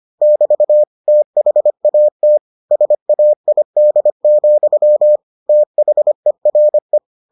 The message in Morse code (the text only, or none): <BT> THAT SAID, THERE